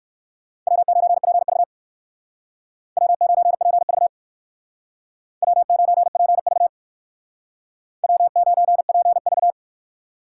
Morse code, heard as a keyed tone, about 56 wpm, 700 Hz. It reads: W9PV W9PV W9PV W9PV